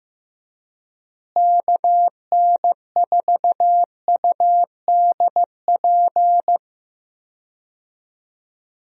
Morse code, heard as a keyed tone, about 15 wpm, 700 Hz. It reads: KN4UDP
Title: USA Call Signs - 6 Characters - Repeated 3x - 15wpm